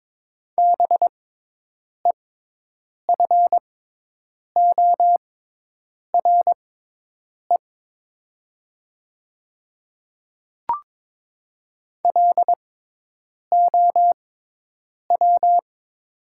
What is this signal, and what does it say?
Morse code, heard as a keyed tone, about 22 wpm, spaced as for 8 wpm, 700 Hz. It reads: BEFORE LOW
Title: 6x Character Spacing - Top 200 Words - 22wpm